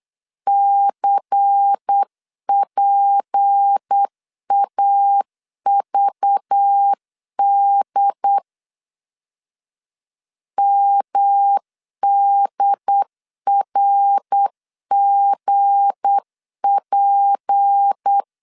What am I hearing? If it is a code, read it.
Morse code, heard as a keyed tone, about 8 wpm, 800 Hz. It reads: CPAVD MDRGP
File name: LEZIONI / 7 GF9Z / 7 G 8wpm 3118